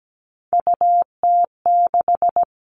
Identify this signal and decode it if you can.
Morse code, heard as a keyed tone, about 17 wpm, 700 Hz. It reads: UT6
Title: All Cut Numbers - 3-Digit 17wpm